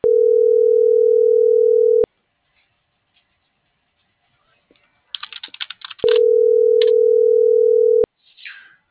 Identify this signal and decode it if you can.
Ringback tone (North America), 440 + 480 Hz.